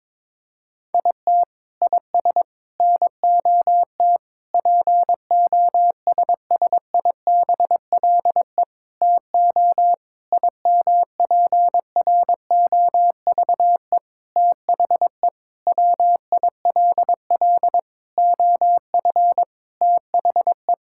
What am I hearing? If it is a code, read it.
Morse code, heard as a keyed tone, about 22 wpm, 700 Hz. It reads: IT IS NOT POSSIBLE TO IMPROVE THE WILL OF THE
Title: Sentences from Top 1000 Words - Review Part 5 - 22wpm